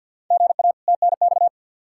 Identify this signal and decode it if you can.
Morse code, heard as a keyed tone, about 51 wpm, 700 Hz. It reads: GA TNX
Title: Practice QSO - Exchange 1 - Beginning - 50wpm